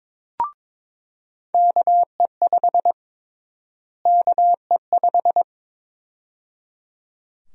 Morse code, heard as a keyed tone, about 22 wpm, 700 Hz. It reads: KE5 KE5
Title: USA Call Sign Prefixes - Repeated 2x - 22wpm